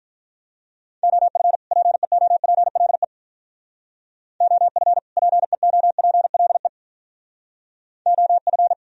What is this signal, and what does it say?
Morse code, heard as a keyed tone, about 45 wpm, 700 Hz. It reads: OF PEOPLE OF PEOPLE OF